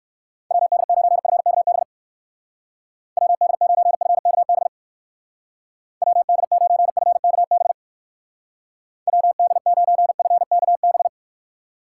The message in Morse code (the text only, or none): WD9FKB WD9FKB WD9FKB WD9FKB